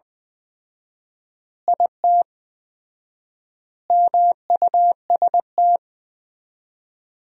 Morse code, heard as a keyed tone, about 20 wpm, 700 Hz. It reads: IT MUST
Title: 4x Word-Spacing - Sentences from Top 500 Words - 20wpm